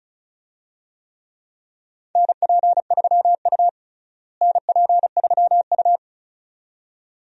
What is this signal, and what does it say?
Morse code, heard as a keyed tone, about 36 wpm, 700 Hz. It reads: NP3U NP3U
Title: USA Call Signs - 4 Characters - Repeated 2x - 35wpm